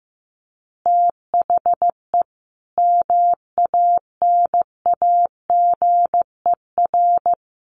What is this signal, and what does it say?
Morse code, heard as a keyed tone, about 15 wpm, 700 Hz. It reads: THE MANAGER
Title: Additional Easy Phrases 15wpm